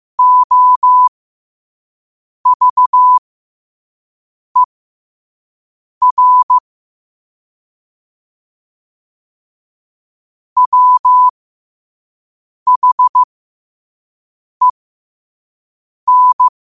Morse code, heard as a keyed tone, about 15 wpm, spaced as for 5 wpm, 1000 Hz. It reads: OVER WHEN
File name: qotdfiles / 5wpm